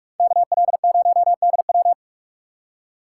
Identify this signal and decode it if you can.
Morse code, heard as a keyed tone, about 59 wpm, 700 Hz. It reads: KL0DW